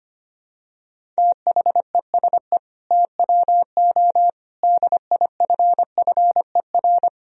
Morse code, heard as a keyed tone, about 25 wpm, 700 Hz. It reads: THESE TWO DIFFER
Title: Sentences from Top 1000 Words - Review Part 1 - 25wpm